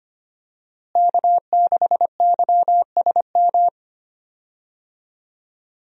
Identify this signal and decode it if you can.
Morse code, heard as a keyed tone, about 25 wpm, 700 Hz. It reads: K6YSM
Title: USA Call Signs - 5 Characters - Repeated 3x - 25wpm